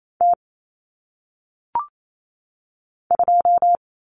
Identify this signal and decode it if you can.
Morse code, heard as a keyed tone, about 28 wpm, 700 Hz. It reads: T 2
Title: Common Cut Numbers - 1-Digit 28wpm